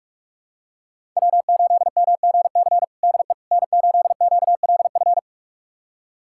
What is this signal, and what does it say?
Morse code, heard as a keyed tone, about 46 wpm, 700 Hz. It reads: W8MGC DE N8QLF
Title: USA Callsign DE Callsign - 5 Characters - 45wpm